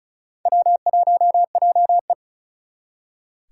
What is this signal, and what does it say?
Morse code, heard as a keyed tone, about 48 wpm, spaced as for 42 wpm, 700 Hz. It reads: W1JE
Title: Speed Racing - USA 1x2 Call Signs 35wpm